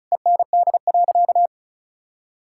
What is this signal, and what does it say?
Morse code, heard as a keyed tone, about 36 wpm, 700 Hz. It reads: END.
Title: Sentences from Top 1600 Words 35wpm